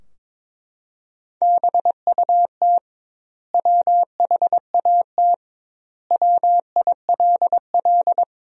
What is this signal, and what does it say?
Morse code, heard as a keyed tone, about 22 wpm, 700 Hz. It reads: BUT WHAT WILL
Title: 2x Word-Spacing - Sentences from Top 100 Words - 22wpm